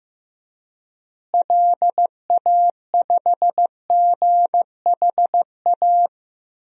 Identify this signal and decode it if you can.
Morse code, heard as a keyed tone, about 15 wpm, 700 Hz. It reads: LA5GHA
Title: DXCC-100 Call Signs - 6 Characters - Repeated 3x - 15wpm